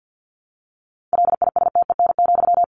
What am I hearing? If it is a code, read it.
Morse code, heard as a keyed tone, about 50 wpm, 700 Hz. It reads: LISTEN,
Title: Sentences from Top 600 Words 50wpm